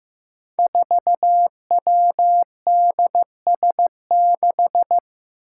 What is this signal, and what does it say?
Morse code, heard as a keyed tone, about 15 wpm, 700 Hz. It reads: 4WDS6
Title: Five Challenging Letter-Number Combinations 15wpm